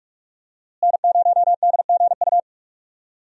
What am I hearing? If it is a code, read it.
Morse code, heard as a keyed tone, about 46 wpm, 700 Hz. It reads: N0DGU
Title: USA Call Signs - 5 Characters - Repeated 3x - 45wpm